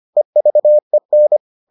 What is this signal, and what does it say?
Morse code, heard as a keyed tone, about 25 wpm, 600 Hz. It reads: EVEN